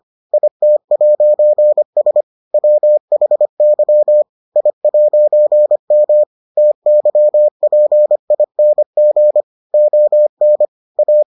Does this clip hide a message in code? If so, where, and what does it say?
Yes: Morse code from 0.3 to 11.2 s, heard as a keyed tone, about 25 wpm, 600 Hz. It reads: IT'S WHY I'M TYPING ON A